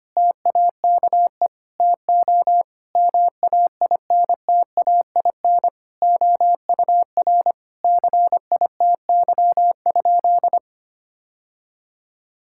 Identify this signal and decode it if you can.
Morse code, heard as a keyed tone, about 25 wpm, 700 Hz. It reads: TAKE TO MAINTAIN OUR CITY?
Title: Sentences from Top 1000 Words - Review Part 5 - 25wpm